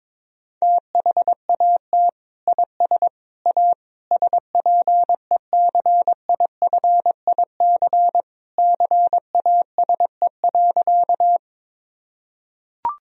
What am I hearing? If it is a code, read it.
Morse code, heard as a keyed tone, about 21 wpm, 700 Hz. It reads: THAT IS A SPECIFIC CASE.
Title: Sentences from Top 1000 Words - Review Part 3 - 22wpm